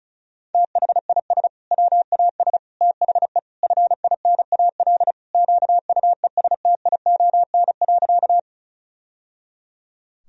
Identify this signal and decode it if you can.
Morse code, heard as a keyed tone, about 36 wpm, 700 Hz. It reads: THIS WAS THE FINAL QUESTION.
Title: Sentences from Top 1000 Words - Review Part 2 - 35wpm